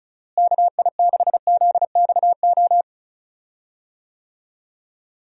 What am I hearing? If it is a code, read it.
Morse code, heard as a keyed tone, about 36 wpm, 700 Hz. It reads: KI6ZXO
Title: USA Call Signs - 6 Characters - Repeated 3x - 35wpm